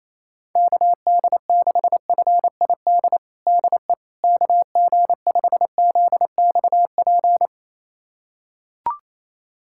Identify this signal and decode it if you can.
Morse code, heard as a keyed tone, about 28 wpm, 700 Hz. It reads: KD6FID DE KG5ZXP
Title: USA Callsign DE Callsign - 6 Characters - 28wpm